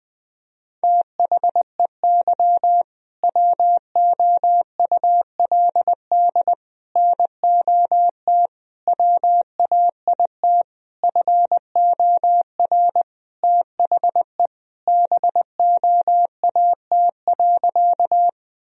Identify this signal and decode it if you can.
Morse code, heard as a keyed tone, about 19 wpm, 700 Hz. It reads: THEY WOULD NOT WAIT FOR THE BOAT.
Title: Sentences from Top 1000 Words - Review Part 3 - 20wpm